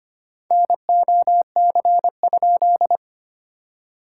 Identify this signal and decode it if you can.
Morse code, heard as a keyed tone, about 25 wpm, 700 Hz. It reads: NOC?
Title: SOTA QSO Practice 25wpm